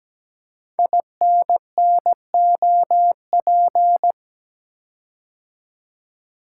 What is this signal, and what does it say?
Morse code, heard as a keyed tone, about 17 wpm, 700 Hz. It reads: INNOP
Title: Code Groups - 5 Letters - 17wpm